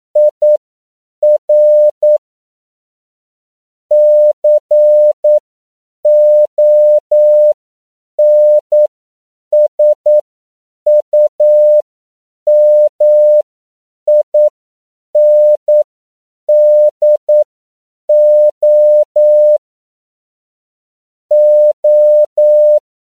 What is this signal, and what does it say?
Morse code, heard as a keyed tone, about 9 wpm, spaced as for 7 wpm, 600 Hz. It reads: IR CONSUMINDO O